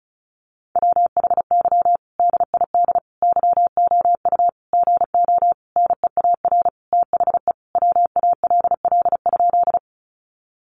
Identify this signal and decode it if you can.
Morse code, heard as a keyed tone, about 34 wpm, 700 Hz. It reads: WHY DID YOU GO NEAR THE WALL?
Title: Sentences from Top 1100 Words 35wpm